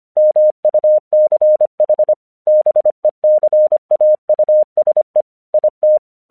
Morse code, heard as a keyed tone, about 25 wpm, 610 Hz. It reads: MUCH BECAUSE IT
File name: 1 English 25wpm / AQC12